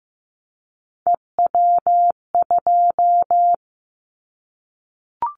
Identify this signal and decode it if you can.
Morse code, heard as a keyed tone, about 15 wpm, 700 Hz. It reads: EW2
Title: Callsign Prefixes for Top 100 DXCC - 15wpm